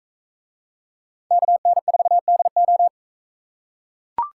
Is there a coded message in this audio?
Yes: Morse code from 1.3 to 2.9 s, heard as a keyed tone, about 42 wpm, 700 Hz. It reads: KN4DO